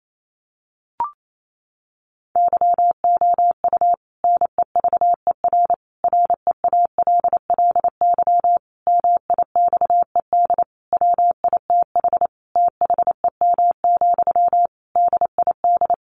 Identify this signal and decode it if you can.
Morse code, heard as a keyed tone, about 28 wpm, 700 Hz. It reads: YOU NEVER REALLY MIXED WITH THEM, DID
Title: Sentences from Top 1500 Words 28wpm